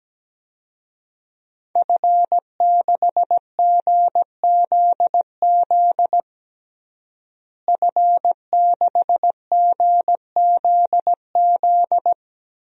Morse code, heard as a keyed tone, about 17 wpm, 700 Hz. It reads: F6GZZ F6GZZ
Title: DXCC-100 Call Signs - 5 Characters - Repeated 3x - 17wpm